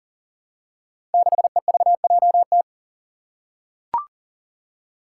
Morse code, heard as a keyed tone, about 40 wpm, 700 Hz. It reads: 6EVJT
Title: Five Challenging Letter-Number Combinations 40wpm